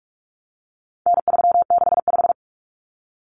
Morse code, heard as a keyed tone, about 44 wpm, 700 Hz. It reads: N365
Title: Common Cut Numbers - 4-Digit 45wpm